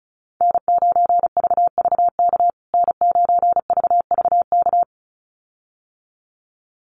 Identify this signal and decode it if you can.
Morse code, heard as a keyed tone, about 34 wpm, 700 Hz. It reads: N9VVK N9VVK
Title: USA Call Signs Repeated 2x 35wpm